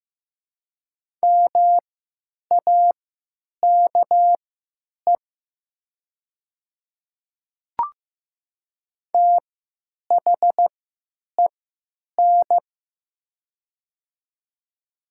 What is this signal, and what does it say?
Morse code, heard as a keyed tone, about 15 wpm, spaced as for 9 wpm, 700 Hz. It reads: MAKE THEN
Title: 3x Character Spacing - Top 100 Words - 15wpm